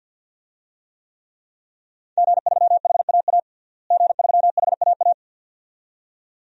Morse code, heard as a keyed tone, about 51 wpm, 700 Hz. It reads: G3HAA G3HAA
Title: DXCC-100 Call Signs - 5 Characters - Repeated 2x - 50wpm